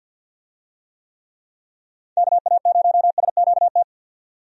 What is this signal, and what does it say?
Morse code, heard as a keyed tone, about 51 wpm, 700 Hz. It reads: KA0SQT
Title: USA Call Signs - 6 Characters - 50wpm